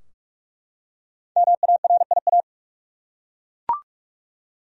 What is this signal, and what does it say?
Morse code, heard as a keyed tone, about 41 wpm, 700 Hz. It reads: MARIA
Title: USA Common Names - 45wpm